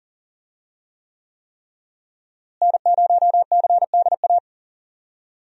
Morse code, heard as a keyed tone, about 40 wpm, 700 Hz. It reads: N0CDA
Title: USA Call Signs - 5 Characters - Repeated 3x - 40wpm